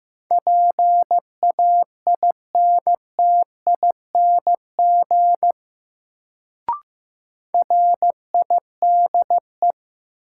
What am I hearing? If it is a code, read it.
Morse code, heard as a keyed tone, about 15 wpm, 700 Hz. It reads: PAINTING RIDE
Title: Sentences from Top 1200 Words - New Word Drill Alt - 15wpm